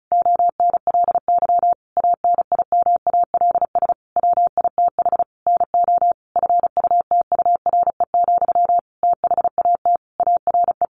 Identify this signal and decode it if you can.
Morse code, heard as a keyed tone, about 34 wpm, 700 Hz. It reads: ONLY ANIMALS WITH NO FUTURE, THAT ARE
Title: A Sound Of Thunder 35wpm